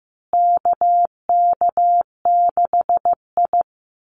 Morse code, heard as a keyed tone, about 14 wpm, 700 Hz. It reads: KK6I
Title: Call Signs Repeated 2x - 15wpm